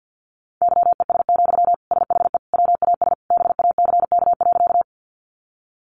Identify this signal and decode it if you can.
Morse code, heard as a keyed tone, about 50 wpm, 700 Hz. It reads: YES, SHE WAS BACK.
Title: Sentences from Top 500 Words 50wpm